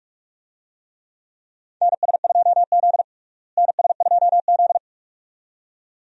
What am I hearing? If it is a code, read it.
Morse code, heard as a keyed tone, about 46 wpm, 700 Hz. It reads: NS2Z NS2Z
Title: USA Call Signs - 4 Characters - Repeated 2x - 45wpm